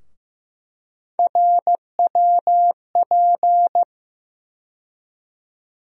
Morse code, heard as a keyed tone, about 15 wpm, 700 Hz. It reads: RWP
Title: USA Call Sign Postfixes - 3 Letters 15wpm